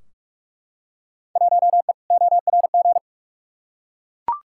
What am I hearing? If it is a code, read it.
Morse code, heard as a keyed tone, about 41 wpm, 700 Hz. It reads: 1E ORG
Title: Field Day Exchanges - 45wpm